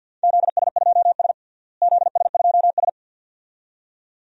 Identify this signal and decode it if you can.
Morse code, heard as a keyed tone, about 47 wpm, 700 Hz. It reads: ZS2S ZS2S
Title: DXCC-100 Call Signs - 4 Characters - Repeated 2x - 50wpm